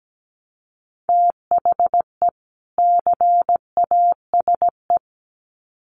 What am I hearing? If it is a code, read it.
Morse code, heard as a keyed tone, about 17 wpm, 700 Hz. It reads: THE CASE
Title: Sets of 2 Words 17wpm